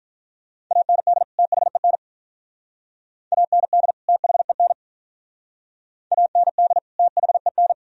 Morse code, heard as a keyed tone, about 46 wpm, 700 Hz. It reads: AND THEN AND THEN AND THEN